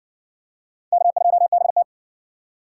Morse code, heard as a keyed tone, about 54 wpm, 700 Hz. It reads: K2BT